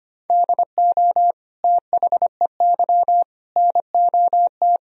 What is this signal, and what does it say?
Morse code, heard as a keyed tone, about 25 wpm, 700 Hz. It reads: DO THEY NOT
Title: Sentences from Top 1000 Words - Review Part 4 - 25wpm